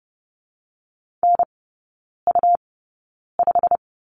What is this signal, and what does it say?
Morse code, heard as a keyed tone, about 30 wpm, 700 Hz. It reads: N U 5
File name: Lesson 19e - Letter w - Three Characters 30wpm